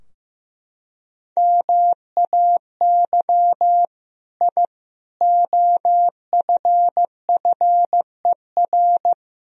Morse code, heard as a keyed tone, about 15 wpm, 700 Hz. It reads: MAY I OFFER
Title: Sentences from Top 1000 Words - Review Part 4 - 15wpm